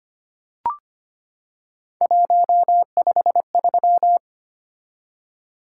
Morse code, heard as a keyed tone, about 25 wpm, 700 Hz. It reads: 153
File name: Three Letter-Number Combinations - 25wpm